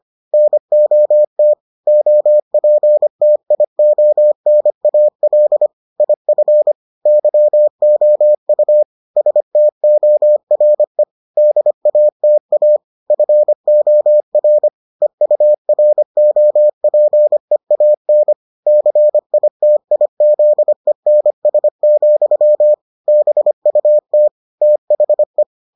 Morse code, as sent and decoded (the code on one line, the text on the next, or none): -. --- - / --- .--. - .. --- -. .- .-.. / .. ..-. / -.-- --- ..- / ... - --- .-. / -.. .- - .- / ..-. --- .-. / . ..- .-. --- .--. .- -. / -.-. .. - .. --.. -. ... --..-- / -... ..- - / - .... .
NOT OPTIONAL IF YOU STORE DATA FOR EUROPEAN CITIZENS, BUT THE